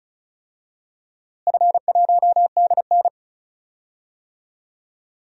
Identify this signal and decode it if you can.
Morse code, heard as a keyed tone, about 36 wpm, 700 Hz. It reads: F1DN
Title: DXCC-100 Call Signs - 4 Characters - Repeated 3x - 35wpm